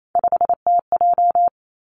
Morse code, heard as a keyed tone, about 28 wpm, 690 Hz. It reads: 5TJ
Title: Three Challenging Letter-Number Combinations 28wpm